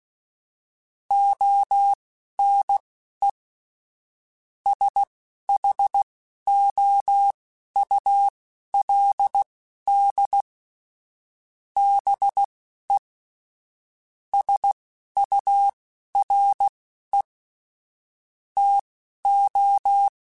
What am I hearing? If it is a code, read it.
Morse code, heard as a keyed tone, about 16 wpm, spaced as for 12 wpm, 800 Hz. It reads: ONE SHOULD BE SURE TO